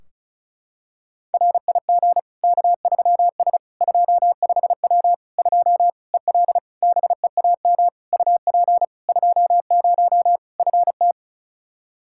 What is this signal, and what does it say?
Morse code, heard as a keyed tone, about 36 wpm, 700 Hz. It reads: RIG K3S 25W 2 EL BEAM UP 20 FT
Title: Practice QSO - Exchange 2 - Info - 35wpm